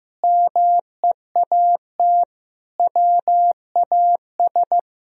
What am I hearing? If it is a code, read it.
Morse code, heard as a keyed tone, about 16 wpm, 700 Hz. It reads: MEAT WAS
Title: Sentences from Top 1000 Words - Review Part 3 - 15wpm